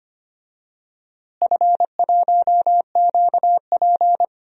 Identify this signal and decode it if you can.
Morse code, heard as a keyed tone, about 25 wpm, 700 Hz. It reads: F1QP
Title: DXCC-100 Call Signs - 4 Characters - Repeated 2x - 25wpm